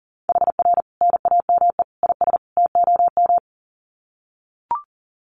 Morse code, heard as a keyed tone, about 40 wpm, 700 Hz. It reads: HR NAME IS TOM
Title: Common QSO Phrases 40wpm